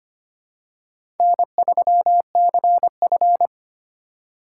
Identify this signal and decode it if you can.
Morse code, heard as a keyed tone, about 25 wpm, 700 Hz. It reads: N3CF
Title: USA Call Signs - 4 Characters - 4x Character Spacing - 25wpm